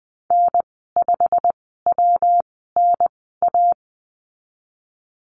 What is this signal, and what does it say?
Morse code, heard as a keyed tone, about 20 wpm, spaced as for 15 wpm, 700 Hz. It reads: N5WNA
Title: 147 Lesson 24g - Letter y - Call Signs 20wpm